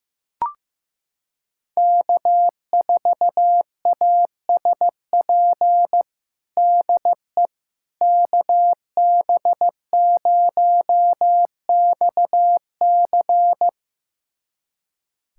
Morse code, heard as a keyed tone, about 15 wpm, 700 Hz. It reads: K4ASP DE KB0XC
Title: USA Callsign DE Callsign - 5 Characters - 15wpm